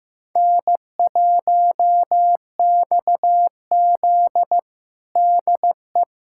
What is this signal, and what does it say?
Morse code, heard as a keyed tone, about 15 wpm, 700 Hz. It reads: N1XZ DE
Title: USA Callsign DE Callsign - 4 Characters - 15wpm